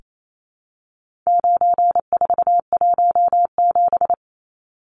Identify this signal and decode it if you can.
Morse code, heard as a keyed tone, about 28 wpm, 700 Hz. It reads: 9417
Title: Numbers - 4-Digit 28wpm